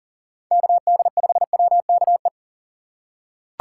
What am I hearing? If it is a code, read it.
Morse code, heard as a keyed tone, about 40 wpm, 700 Hz. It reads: KD5WKE
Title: USA Call Signs - 6 Characters - 2x Character Spacing - 40wpm